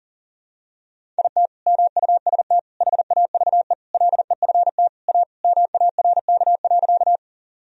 Morse code, heard as a keyed tone, about 40 wpm, 700 Hz. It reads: IT MUST HAVE LEFT A MARK.
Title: Sentences from Top 1000 Words - Review Part 1 - 40wpm